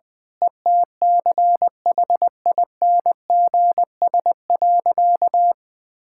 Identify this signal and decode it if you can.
Morse code, heard as a keyed tone, about 20 wpm, 700 Hz. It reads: ETCHINGS.
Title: Pangram Challenge v2 - 20wpm